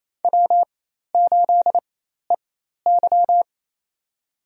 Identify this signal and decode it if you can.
Morse code, heard as a keyed tone, about 28 wpm, spaced as for 13 wpm, 700 Hz. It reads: W8EY
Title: USA Call Signs - 4 Characters - 4x Character Spacing - 28wpm